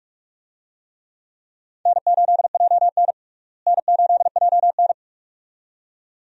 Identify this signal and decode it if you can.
Morse code, heard as a keyed tone, about 44 wpm, 700 Hz. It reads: N8JN N8JN
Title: USA Call Signs - 4 Characters - Repeated 2x - 45wpm